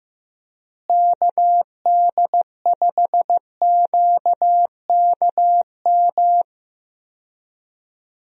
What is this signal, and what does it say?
Morse code, heard as a keyed tone, about 16 wpm, 700 Hz. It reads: KD5QKM
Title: USA Call Signs - 6 Characters - Repeated 3x - 15wpm